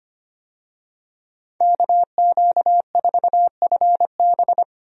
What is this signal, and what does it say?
Morse code, heard as a keyed tone, about 25 wpm, 700 Hz. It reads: KQ4FB
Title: USA Call Signs - 5 Characters - 25wpm